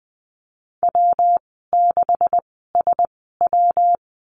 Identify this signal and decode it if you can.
Morse code, heard as a keyed tone, about 20 wpm, spaced as for 15 wpm, 700 Hz. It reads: W6SW
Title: 140 Lesson 23g - Letter f - Call Signs 20wpm